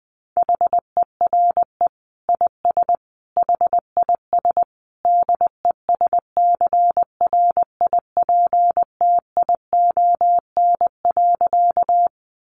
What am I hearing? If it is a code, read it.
Morse code, heard as a keyed tone, about 20 wpm, 700 Hz. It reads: HERE IS HIS DESCRIPTION.
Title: Sentences from Top 1300 Words 20wpm